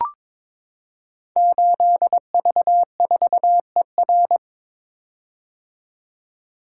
Morse code, heard as a keyed tone, about 22 wpm, 700 Hz. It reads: 8V4ER
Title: Code Groups - 5 Letter:Numbers - 22wpm